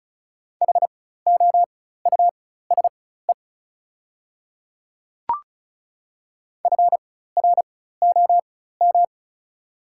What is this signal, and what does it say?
Morse code, heard as a keyed tone, about 36 wpm, spaced as for 16 wpm, 700 Hz. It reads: HOUSE FROM
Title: 4x Character Spacing - Top 200 Words - 35wpm